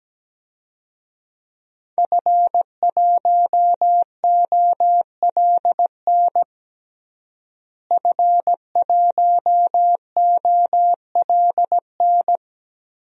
Morse code, heard as a keyed tone, about 17 wpm, 700 Hz. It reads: F1OLN F1OLN
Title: DXCC-100 Call Signs - 5 Characters - Repeated 3x - 17wpm